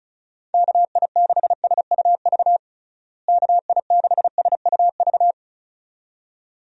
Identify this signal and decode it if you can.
Morse code, heard as a keyed tone, about 34 wpm, 700 Hz. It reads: KI6SUV KI6SUV
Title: USA Call Signs - 6 Characters - Repeated 2x - 35wpm